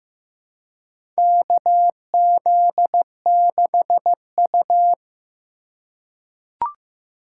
Morse code, heard as a keyed tone, about 15 wpm, 700 Hz. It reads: KZ6U
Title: USA Call Signs - 4 Characters - 4x Character Spacing - 15wpm